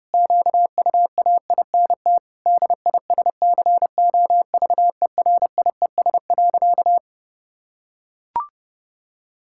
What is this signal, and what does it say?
Morse code, heard as a keyed tone, about 30 wpm, 700 Hz. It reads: QUAINT DISCOVERIES.
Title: Pangram Challenge v2 - 30wpm